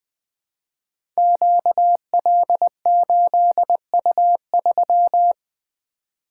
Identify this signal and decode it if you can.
Morse code, heard as a keyed tone, about 20 wpm, 700 Hz. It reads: QL8U3
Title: Code Groups - 5 Letter:Numbers - 20wpm